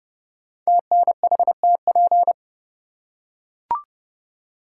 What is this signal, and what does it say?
Morse code, heard as a keyed tone, about 30 wpm, 700 Hz. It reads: TNHTP
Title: Code Groups - 5 Letters - 30wpm